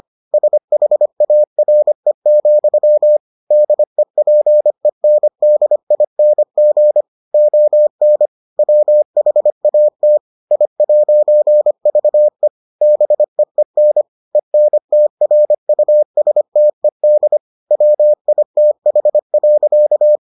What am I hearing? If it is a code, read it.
Morse code, heard as a keyed tone, about 25 wpm, 600 Hz. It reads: SHARE, DEPENDING ON WHAT I'VE BEEN ENTRUSTED WITH.